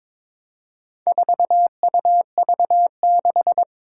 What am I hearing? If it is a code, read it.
Morse code, heard as a keyed tone, about 22 wpm, 700 Hz. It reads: 4UV6
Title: Four Challenging Letter-Number Combinations 22wpm